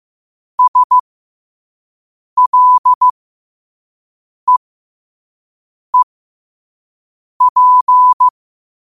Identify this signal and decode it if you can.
Morse code, heard as a keyed tone, about 15 wpm, spaced as for 5 wpm, 1000 Hz. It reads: SLEEP
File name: qotdfiles / 5wpm